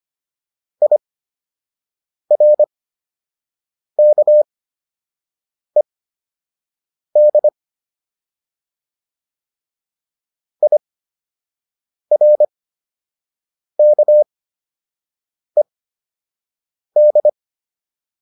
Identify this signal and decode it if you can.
Morse code, heard as a keyed tone, about 25 wpm, spaced as for 6 wpm, 600 Hz. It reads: IRKED IRKED